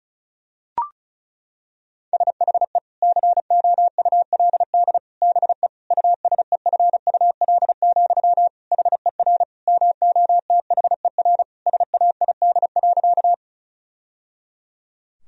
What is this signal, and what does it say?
Morse code, heard as a keyed tone, about 36 wpm, 700 Hz. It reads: SHE COULD BE USEFUL, HER MOTHER SAID.
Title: Sentences from Top 1000 Words - Review Part 4 - 35wpm